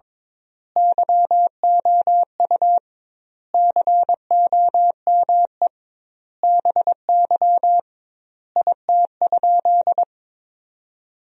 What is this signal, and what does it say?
Morse code, heard as a keyed tone, about 22 wpm, 700 Hz. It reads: YOU COME BY IT?
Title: 2x Word-Spacing - Sentences from Top 100 Words - 22wpm